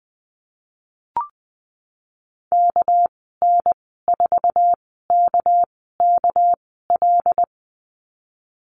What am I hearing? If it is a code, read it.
Morse code, heard as a keyed tone, about 19 wpm, spaced as for 14 wpm, 700 Hz. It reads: KN4KKL
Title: 238 Lesson 38g - Letter q - Call Signs 20wpm